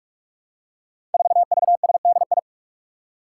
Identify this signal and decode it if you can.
Morse code, heard as a keyed tone, about 44 wpm, 700 Hz. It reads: 4VSDI
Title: Five Challenging Letter-Number Combinations 45wpm